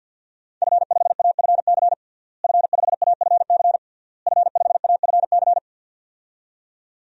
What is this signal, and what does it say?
Morse code, heard as a keyed tone, about 52 wpm, 700 Hz. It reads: F5AFC F5AFC F5AFC
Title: DXCC-100 Call Signs - 5 Characters - Repeated 3x - 50wpm